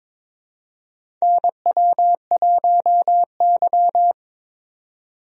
Morse code, heard as a keyed tone, about 23 wpm, 700 Hz. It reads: NW1Y
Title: USA Call Signs - 4 Characters - Repeated 2x - 22wpm